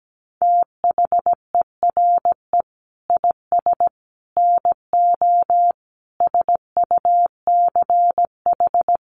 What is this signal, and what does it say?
Morse code, heard as a keyed tone, about 17 wpm, 700 Hz. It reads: THERE IS NO SUCH
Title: Sentences from Top 1400 Words 17wpm